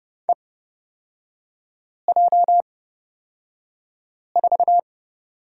Morse code, heard as a keyed tone, about 30 wpm, spaced as for 5 wpm, 700 Hz. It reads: EJ4
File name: Lesson 36e - BT - Three Characters 30wpm